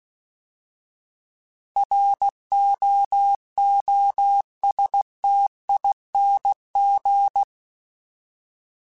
Morse code, heard as a keyed tone, about 16 wpm, 800 Hz. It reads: ROOSTING
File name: Alphabetical / Alphabetical mp3s / Rs @15